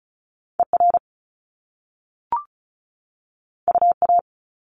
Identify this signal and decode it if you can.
Morse code, heard as a keyed tone, about 36 wpm, 700 Hz. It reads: ER UA
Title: Top Two Letter Combinations in Words 35wpm